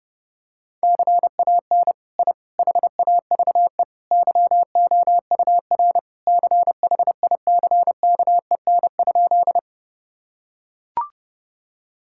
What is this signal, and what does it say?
Morse code, heard as a keyed tone, about 30 wpm, 700 Hz. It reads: CAN I HAVE YOUR CHICKEN?
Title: Sentences from Top 1000 Words - Review Part 5 - 30wpm